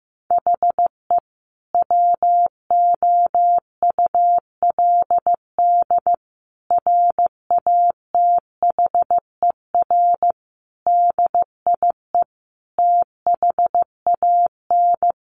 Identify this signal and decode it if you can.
Morse code, heard as a keyed tone, about 15 wpm, 700 Hz. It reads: HE WOULD RATHER DIE THAN
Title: Sentences from Top 1500 Words 15wpm